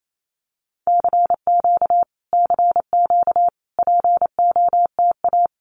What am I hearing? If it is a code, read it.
Morse code, heard as a keyed tone, about 28 wpm, 700 Hz. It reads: CQ CQ POTA
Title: Common QSO Phrases Encore 28wpm